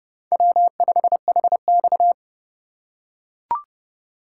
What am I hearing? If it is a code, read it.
Morse code, heard as a keyed tone, about 30 wpm, 700 Hz. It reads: W5HX
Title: USA Call Signs - 4 Characters - 6x Character Spacing - 30wpm